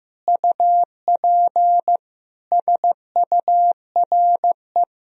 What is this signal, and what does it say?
Morse code, heard as a keyed tone, about 15 wpm, 700 Hz. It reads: UP SURE
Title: Sentences from Top 1000 Words - Review Part 1 - 15wpm